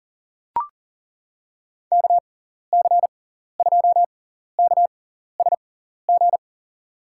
Morse code, heard as a keyed tone, about 40 wpm, spaced as for 14 wpm, 700 Hz. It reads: KC2KSG
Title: USA Call Signs - 6 Characters - 6x Character Spacing - 40wpm